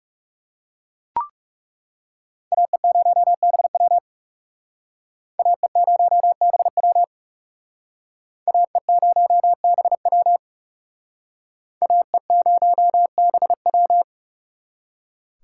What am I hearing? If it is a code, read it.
Morse code, heard as a keyed tone, about 37 wpm, 700 Hz. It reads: AE0BW AE0BW AE0BW AE0BW
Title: Speed Racing - USA Call Signs 30wpm